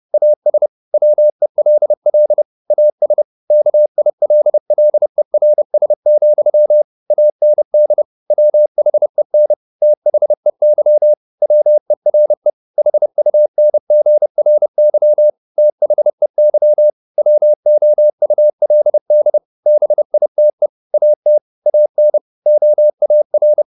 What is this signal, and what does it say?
Morse code, heard as a keyed tone, about 30 wpm, 600 Hz. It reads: AS WELL AS KILLERS, AND WHEN THEY WERE HUNGRY THEY WOULD BITE AT AN OAR